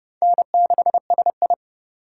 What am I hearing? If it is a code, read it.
Morse code, heard as a keyed tone, about 30 wpm, 700 Hz. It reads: N6SI